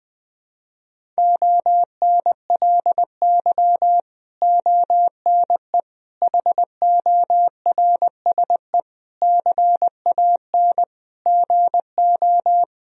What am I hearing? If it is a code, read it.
Morse code, heard as a keyed tone, about 20 wpm, 700 Hz. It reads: ONLY ONE HORSE CAN GO
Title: Sentences from Top 1000 Words - Review Part 1 - 20wpm